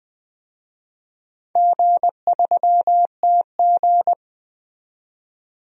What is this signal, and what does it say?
Morse code, heard as a keyed tone, about 21 wpm, 700 Hz. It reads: G3TG
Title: DXCC-100 Call Signs - 4 Characters - 20wpm